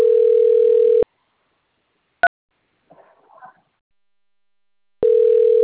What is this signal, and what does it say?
Ringback tone (North America), 440 + 480 Hz.